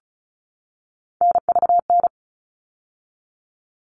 Morse code, heard as a keyed tone, about 34 wpm, 700 Hz. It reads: NVN
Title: All Cut Numbers - 3-Digit 35wpm